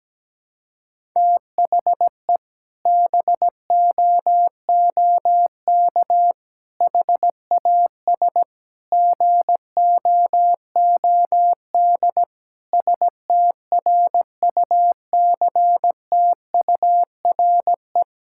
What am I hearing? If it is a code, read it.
Morse code, heard as a keyed tone, about 17 wpm, 700 Hz. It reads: THE BOOK HAS GOOD STRUCTURE
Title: Sentences from Top 1000 Words - Review Part 6 - 17wpm